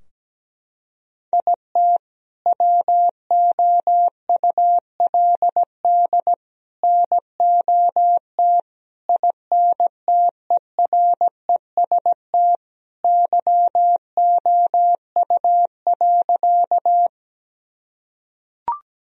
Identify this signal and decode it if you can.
Morse code, heard as a keyed tone, about 17 wpm, 700 Hz. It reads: IT WOULD NOT INTEREST YOU.
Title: Sentences from Top 1000 Words - Review Part 2 - 17wpm